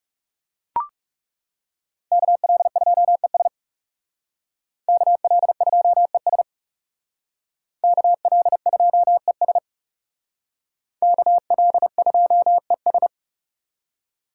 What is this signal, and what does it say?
Morse code, heard as a keyed tone, about 38 wpm, 700 Hz. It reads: KL2ES KL2ES KL2ES KL2ES